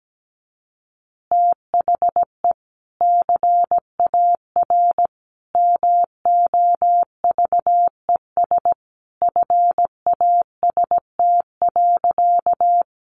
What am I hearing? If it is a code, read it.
Morse code, heard as a keyed tone, about 17 wpm, 700 Hz. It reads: THE CAR MOVES FAST.
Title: Easy Phrases 17wpm